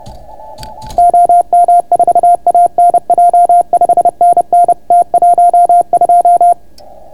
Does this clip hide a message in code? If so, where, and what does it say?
Yes: Morse code from 1.0 to 6.5 s, heard as a keyed tone, about 30 wpm, 690 Hz. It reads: OM4ANJ 5NN T12